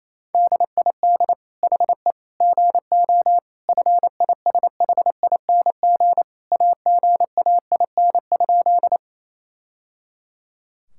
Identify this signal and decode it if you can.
Morse code, heard as a keyed tone, about 28 wpm, 700 Hz. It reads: DID HE GO FISHING AGAIN?
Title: Sentences from Top 1000 Words - Review Part 5 - 28wpm